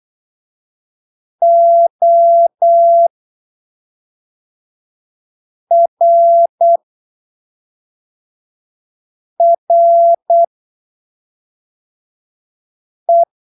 Morse code, heard as a keyed tone, about 8 wpm, spaced as for 3 wpm, 680 Hz. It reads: ORRE